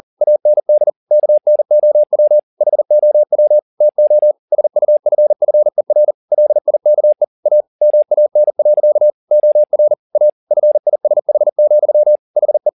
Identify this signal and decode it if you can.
Morse code, heard as a keyed tone, about 40 wpm, 600 Hz. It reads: AND KNOW HOW TO SUFFER LIKE A MAN. OR A FISH, HE
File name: the old man and the sea 2 40wpm